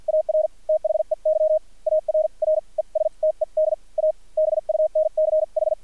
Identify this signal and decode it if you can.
Morse code, heard as a keyed tone, about 50 wpm, spaced as for 40 wpm, 620 Hz, with radio noise. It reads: AR THEY ARRESTED A BUNCH
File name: archived / CRSnet